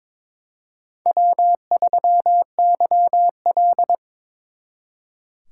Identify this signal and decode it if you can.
Morse code, heard as a keyed tone, about 26 wpm, 700 Hz. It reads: W3YL
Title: Speed Racing - USA 1x2 Call Signs 22wpm